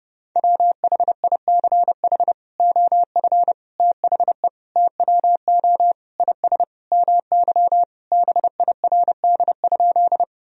Morse code, heard as a keyed tone, about 32 wpm, 700 Hz. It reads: WHICH OF THE TWO IS MY BIRD?
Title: Sentences from Top 1000 Words - Review Part 2 - 30wpm